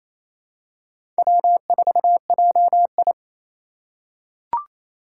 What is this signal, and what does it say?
Morse code, heard as a keyed tone, about 28 wpm, 700 Hz. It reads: W4JI
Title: USA Call Signs - 2x Character Spacing - 28wpm